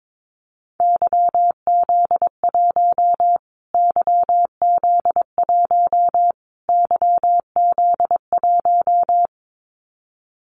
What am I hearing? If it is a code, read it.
Morse code, heard as a keyed tone, about 22 wpm, 700 Hz. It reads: YZ1 YZ1 YZ1
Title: Callsign Prefixes for Top 25 DXCC - Repeated 3x - 22wpm